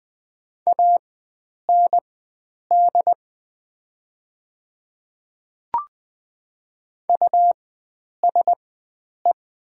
Morse code, heard as a keyed tone, about 20 wpm, spaced as for 9 wpm, 700 Hz. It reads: AND USE
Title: 4x Character Spacing - Top 100 Words - 20wpm